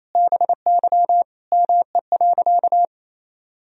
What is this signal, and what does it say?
Morse code, heard as a keyed tone, about 28 wpm, 700 Hz. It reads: BY ME.
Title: Sentences from Top 500 Words and Abbreviations 28wpm